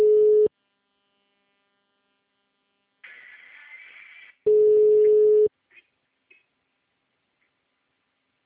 Ringback tone (Europe, 425 Hz).